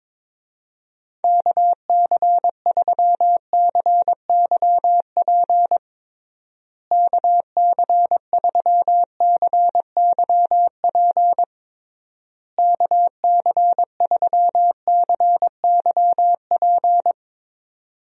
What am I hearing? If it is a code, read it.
Morse code, heard as a keyed tone, about 22 wpm, 700 Hz. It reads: KC3CYP KC3CYP KC3CYP
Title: USA Call Signs - 6 Characters - Repeated 3x - 22wpm